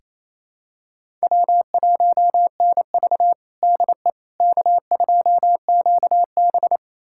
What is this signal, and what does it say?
Morse code, heard as a keyed tone, about 28 wpm, 700 Hz. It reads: W1NV DE K2QB
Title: USA Callsign DE Callsign - 4 Characters - 28wpm